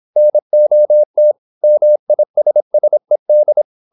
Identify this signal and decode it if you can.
Morse code, heard as a keyed tone, about 26 wpm, 600 Hz. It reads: NOT MISSED
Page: Never 26wpm | CW med Gnister